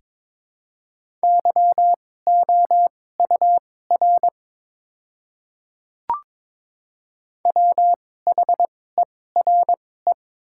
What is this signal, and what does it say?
Morse code, heard as a keyed tone, about 22 wpm, spaced as for 16 wpm, 700 Hz. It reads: YOUR WHERE
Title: 2x Character Spacing - Top 200 Words - 22wpm